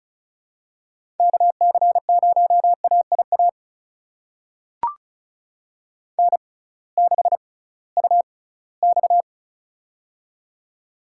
Morse code, heard as a keyed tone, about 36 wpm, spaced as for 20 wpm, 700 Hz. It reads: KC0AIA N6UX
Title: USA Call Signs - 6x Character Spacing - 35wpm